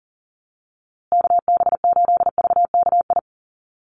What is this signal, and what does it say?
Morse code, heard as a keyed tone, about 37 wpm, 700 Hz. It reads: KB8VKI
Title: USA Call Signs Repeated 3x 40wpm